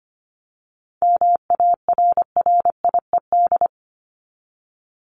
Morse code, heard as a keyed tone, about 25 wpm, 700 Hz. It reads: MARRIED
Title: ED Suffix - Top 100 Words with Base Words 25wpm